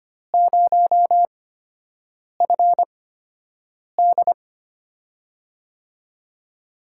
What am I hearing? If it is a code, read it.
Morse code, heard as a keyed tone, about 25 wpm, spaced as for 7 wpm, 700 Hz. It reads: 0FD
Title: USA Call Signs - 8x Character Spacing - 25wpm